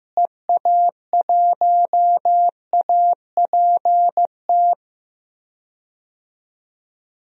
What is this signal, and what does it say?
Morse code, heard as a keyed tone, about 15 wpm, 700 Hz. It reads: EA1APT
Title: DXCC-100 Call Signs - 6 Characters - Repeated 3x - 15wpm